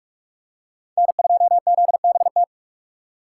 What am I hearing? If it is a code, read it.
Morse code, heard as a keyed tone, about 40 wpm, 700 Hz. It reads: N2ZBT